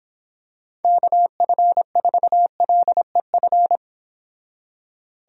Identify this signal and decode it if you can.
Morse code, heard as a keyed tone, about 26 wpm, 700 Hz. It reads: KF4LEF